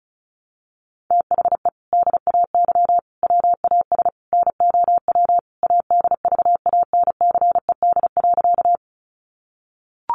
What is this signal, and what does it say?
Morse code, heard as a keyed tone, about 34 wpm, 700 Hz. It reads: THE DAY WAS NOW ADVANCED.
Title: Sentences from Top 1200 Words 35wpm